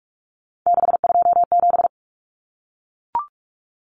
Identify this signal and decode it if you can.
Morse code, heard as a keyed tone, about 44 wpm, 700 Hz. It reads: 627
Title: Common Cut Numbers - 3-Digit 45wpm